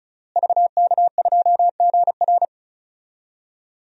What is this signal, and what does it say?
Morse code, heard as a keyed tone, about 36 wpm, 700 Hz. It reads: VK2GR
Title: DXCC-100 Call Signs - 5 Characters - Repeated 3x - 35wpm